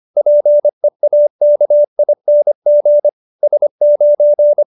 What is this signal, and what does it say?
Morse code, heard as a keyed tone, about 25 wpm, 590 Hz. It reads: PEAKING S9
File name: QSO303 25